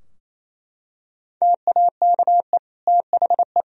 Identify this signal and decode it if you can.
Morse code, heard as a keyed tone, about 28 wpm, 700 Hz. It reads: TAKE THE
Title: Sentences from Top 1000 Words - Review Part 2 - 28wpm